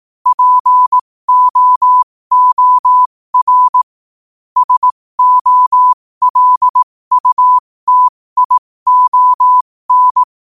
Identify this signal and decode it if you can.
Morse code, heard as a keyed tone, about 18 wpm, spaced as for 15 wpm, 1000 Hz. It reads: POOR SOLUTION
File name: qotdfiles / 15wpm